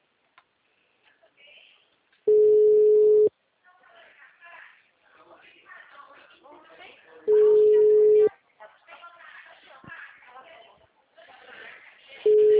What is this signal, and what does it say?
Ringback tone (Europe, 425 Hz), with speech.